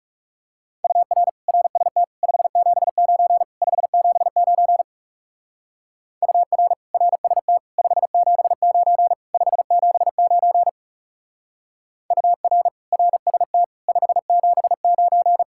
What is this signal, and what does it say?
Morse code, heard as a keyed tone, about 37 wpm, 700 Hz. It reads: UR RST 579 579 UR RST 579 579 UR RST 579
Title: Speed-Racing Common QSO Phrases - 30wpm